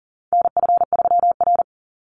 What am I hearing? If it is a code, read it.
Morse code, heard as a keyed tone, about 40 wpm, 700 Hz. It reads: NF3R
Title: Call Signs - 40wpm